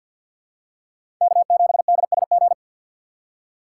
Morse code, heard as a keyed tone, about 52 wpm, 700 Hz. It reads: K7DSG
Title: USA Call Signs - 5 Characters - 2x Character Spacing - 50wpm